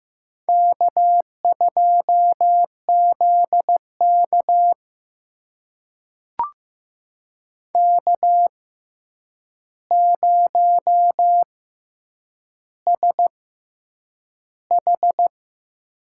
Morse code, heard as a keyed tone, about 15 wpm, spaced as for 8 wpm, 700 Hz. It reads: K2ZK K0SH
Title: USA Call Signs - 4 Characters - 6x Character Spacing - 15wpm